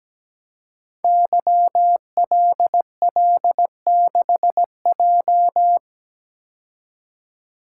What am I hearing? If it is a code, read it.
Morse code, heard as a keyed tone, about 17 wpm, 700 Hz. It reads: YLL6J
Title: Code Groups - 5 Letter:Numbers - 17wpm